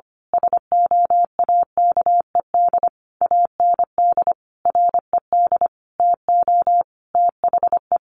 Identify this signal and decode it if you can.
Morse code, heard as a keyed tone, about 25 wpm, 700 Hz. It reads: SOAKED AND RED TO THE
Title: A Sound Of Thunder 25wpm